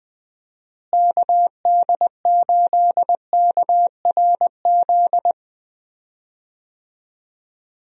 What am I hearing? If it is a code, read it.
Morse code, heard as a keyed tone, about 20 wpm, 700 Hz. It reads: KD8KRZ
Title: USA Call Signs - 6 Characters - Repeated 2x - 20wpm